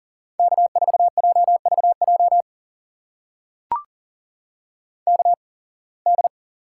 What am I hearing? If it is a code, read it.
Morse code, heard as a keyed tone, about 39 wpm, spaced as for 26 wpm, 700 Hz. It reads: K4JVJ KD
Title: USA Call Signs - 5 Characters - 8x Character Spacing - 40wpm